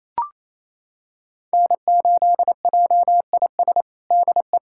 Morse code, heard as a keyed tone, about 28 wpm, 700 Hz. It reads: N8JIS DE
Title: USA Callsign DE Callsign 28wpm